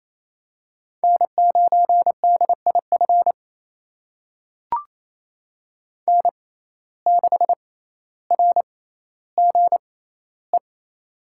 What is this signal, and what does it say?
Morse code, heard as a keyed tone, about 28 wpm, spaced as for 14 wpm, 700 Hz. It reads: N9DIF N6RGE
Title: USA Call Signs - 6x Character Spacing - 28wpm